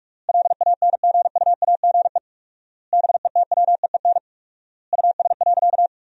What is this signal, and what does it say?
Morse code, heard as a keyed tone, about 47 wpm, 700 Hz. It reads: LANGUAGE BETWEEN US.
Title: 4x Word-Spacing - Sentences from Top 500 Words - 45wpm